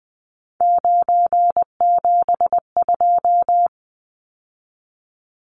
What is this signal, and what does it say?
Morse code, heard as a keyed tone, about 20 wpm, 700 Hz. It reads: 972
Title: Numbers - 3-Digit 20wpm